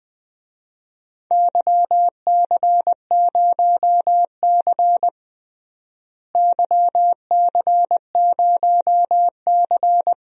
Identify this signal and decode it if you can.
Morse code, heard as a keyed tone, about 21 wpm, 700 Hz. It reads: YC0C YC0C
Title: DXCC-100 Call Signs - 4 Characters - Repeated 2x - 20wpm